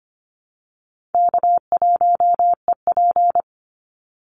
Morse code, heard as a keyed tone, about 25 wpm, 700 Hz. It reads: K1EP
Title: Call Signs - 25wpm